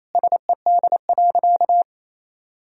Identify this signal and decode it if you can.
Morse code, heard as a keyed tone, about 28 wpm, 700 Hz. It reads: SED.
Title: Sentences from Top 100 Words and Abbreviations 28wpm